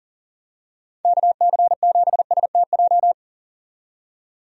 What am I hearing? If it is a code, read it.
Morse code, heard as a keyed tone, about 40 wpm, 700 Hz. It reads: KC7STJ
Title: USA Call Signs - 6 Characters - 40wpm